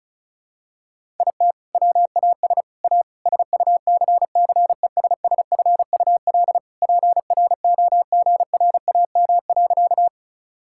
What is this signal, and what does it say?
Morse code, heard as a keyed tone, about 36 wpm, 700 Hz. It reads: IT WAS A SUCCESSFUL PROGRAM.
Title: Sentences from Top 1000 Words - Review Part 5 - 35wpm